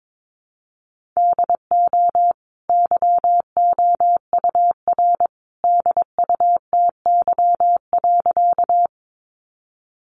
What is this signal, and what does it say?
Morse code, heard as a keyed tone, about 22 wpm, 700 Hz. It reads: DO YOUR DUTY.
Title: Sentences from Top 1500 Words 22wpm